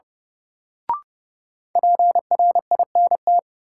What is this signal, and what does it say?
Morse code, heard as a keyed tone, about 30 wpm, 700 Hz. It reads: PRINT
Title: Sentences from Top 1400 Words - New Word Drill - 30wpm